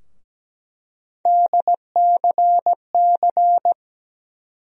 Morse code, heard as a keyed tone, about 17 wpm, 700 Hz. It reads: DCC
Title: USA Call Sign Postfixes - 3 Letters 17wpm